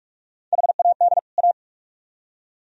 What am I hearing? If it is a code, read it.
Morse code, heard as a keyed tone, about 44 wpm, 700 Hz. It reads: HAD A